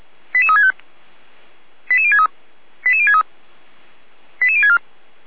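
Five-tone selective call (ZVEI), heard as quick runs of five tones, tones of about 70 ms. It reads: 88365 88063 88063 88064
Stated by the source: Five-tone ZVEI selective call received on SDR.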